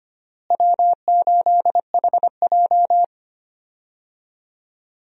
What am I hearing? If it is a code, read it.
Morse code, heard as a keyed tone, about 25 wpm, 700 Hz. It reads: W8HJ
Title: USA Call Signs - 4 Characters - Repeated 2x - 25wpm